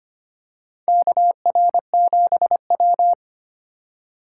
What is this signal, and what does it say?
Morse code, heard as a keyed tone, about 25 wpm, 700 Hz. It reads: KR7W
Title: K1USN SST QSO Practice - 25wpm